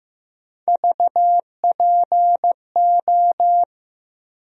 Morse code, heard as a keyed tone, about 16 wpm, 700 Hz. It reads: VPO
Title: USA Call Sign Postfixes - 3 Letters 15wpm